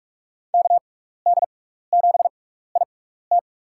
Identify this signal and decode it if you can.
Morse code, heard as a keyed tone, about 46 wpm, spaced as for 16 wpm, 700 Hz. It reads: KD7IT
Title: USA Call Signs - 6x Character Spacing - 45wpm